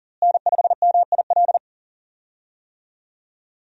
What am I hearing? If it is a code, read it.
Morse code, heard as a keyed tone, about 40 wpm, 700 Hz. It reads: N5MIL
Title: USA Call Signs - 5 Characters - Repeated 3x - 40wpm